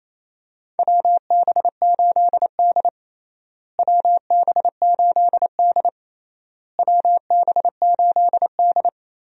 Morse code, heard as a keyed tone, about 28 wpm, 700 Hz. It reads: WB8D WB8D WB8D
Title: USA Call Signs - 4 Characters - Repeated 3x - 28wpm